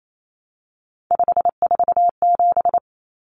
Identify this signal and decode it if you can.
Morse code, heard as a keyed tone, about 28 wpm, 700 Hz. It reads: 547
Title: Numbers - 3-Digit 28wpm